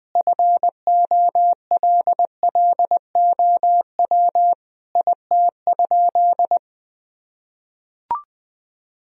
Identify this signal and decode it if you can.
Morse code, heard as a keyed tone, about 21 wpm, 700 Hz. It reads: FOLLOW IT?
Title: Sentences from Top 1000 Words - Review Part 1 - 20wpm